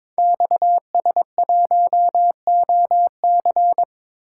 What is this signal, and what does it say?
Morse code, heard as a keyed tone, about 23 wpm, 700 Hz. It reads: XS1OC
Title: Code Groups - 5 Letter:Numbers - 22wpm